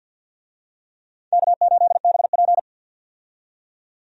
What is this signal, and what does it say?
Morse code, heard as a keyed tone, about 50 wpm, 700 Hz. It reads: K8BP
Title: USA Call Signs - 4 Characters - 50wpm